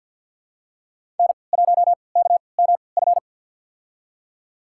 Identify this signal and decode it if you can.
Morse code, heard as a keyed tone, about 52 wpm, spaced as for 28 wpm, 700 Hz. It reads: N1KMF
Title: USA Call Signs - 3x Character Spacing - 50wpm